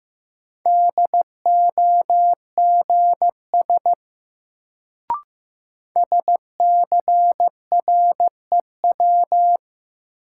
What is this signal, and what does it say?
Morse code, heard as a keyed tone, about 15 wpm, 700 Hz. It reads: DOGS SCREW
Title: Sentences from Top 1600 Words - New Word Drill - 15wpm